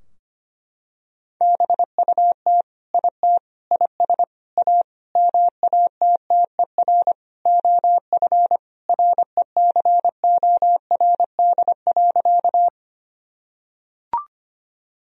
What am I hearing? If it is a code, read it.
Morse code, heard as a keyed tone, about 25 wpm, 700 Hz. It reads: BUT IT IS A MATTER OF RECORD.
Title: Sentences from Top 1000 Words - Review Part 3 - 25wpm